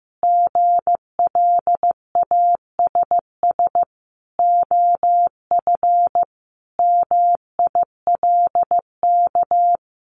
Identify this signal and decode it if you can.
Morse code, heard as a keyed tone, about 15 wpm, 700 Hz. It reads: GLASS OF MILK
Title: Additional Easy Phrases 15wpm